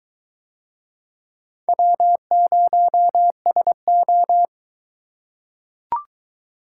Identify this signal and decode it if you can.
Morse code, heard as a keyed tone, about 21 wpm, 700 Hz. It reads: W0SO